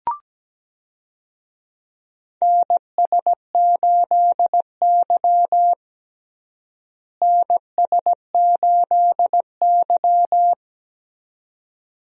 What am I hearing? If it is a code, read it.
Morse code, heard as a keyed tone, about 17 wpm, 700 Hz. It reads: NS8Y NS8Y
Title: USA Call Signs - 4 Characters - Repeated 2x - 17wpm